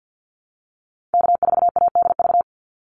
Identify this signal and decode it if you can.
Morse code, heard as a keyed tone, about 50 wpm, 700 Hz. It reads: K4ADV
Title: NA SOTA Activator Callsigns 50wpm